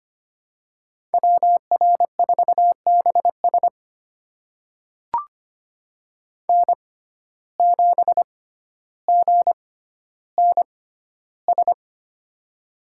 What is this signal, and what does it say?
Morse code, heard as a keyed tone, about 25 wpm, spaced as for 13 wpm, 700 Hz. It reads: WR4BS N7GNS
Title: USA Call Signs - 5 Characters - 6x Character Spacing - 25wpm